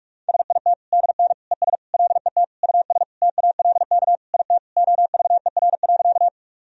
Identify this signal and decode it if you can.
Morse code, heard as a keyed tone, about 46 wpm, 690 Hz. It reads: SIT DN ES LET US TALK IT OVER.
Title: Sentences from Top 500 Words and Abbreviations 45wpm